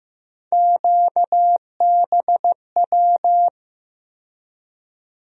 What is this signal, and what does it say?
Morse code, heard as a keyed tone, about 15 wpm, 700 Hz. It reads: QBW
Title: USA Call Sign Postfixes - 3 Letters 15wpm